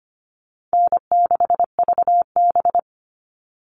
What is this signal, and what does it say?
Morse code, heard as a keyed tone, about 25 wpm, 700 Hz. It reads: N6VB
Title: All Cut Numbers - 4-Digit 25wpm